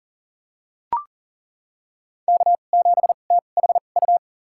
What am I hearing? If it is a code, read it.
Morse code, heard as a keyed tone, about 40 wpm, spaced as for 29 wpm, 700 Hz. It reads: K7THU
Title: USA Call Signs - 5 Characters - 2x Character Spacing - 40wpm